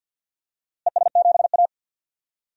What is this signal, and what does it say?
Morse code, heard as a keyed tone, about 50 wpm, 700 Hz. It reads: ES7A
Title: DXCC-100 Call Signs - 4 Characters - Repeated 3x - 50wpm